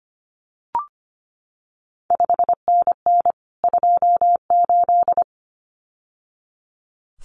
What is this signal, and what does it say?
Morse code, heard as a keyed tone, about 25 wpm, 700 Hz. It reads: 5NN 28
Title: CQ WW DX Contest Exchange Encore - 25wpm